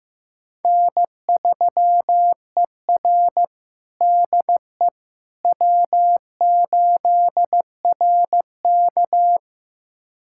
Morse code, heard as a keyed tone, about 15 wpm, 700 Hz. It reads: N3ER DE W8RK
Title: USA Callsign DE Callsign - 4 Characters - 15wpm